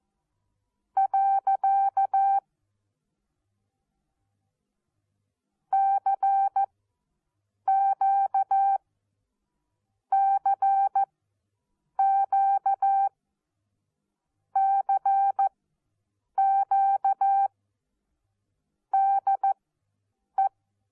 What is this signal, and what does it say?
Morse code, heard as a keyed tone, about 14 wpm, spaced as for 7 wpm, 780 Hz. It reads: . CQ CQ CQ DE
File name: Onlinelehrgang / morsen / DJ4UF Morsen14